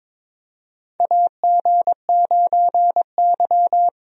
Morse code, heard as a keyed tone, about 22 wpm, 700 Hz. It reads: AG9Y
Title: USA Call Signs - 4 Characters - Repeated 2x - 22wpm